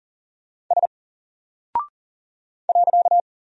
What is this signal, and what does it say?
Morse code, heard as a keyed tone, about 40 wpm, 700 Hz. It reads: S .
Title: Single Character (Letter:Number:Punctuation:Prosign) 40wpm